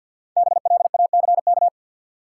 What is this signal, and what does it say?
Morse code, heard as a keyed tone, about 53 wpm, 700 Hz. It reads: BLACK
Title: Sets of 5 Words Encore - New Word Drill - 50wpm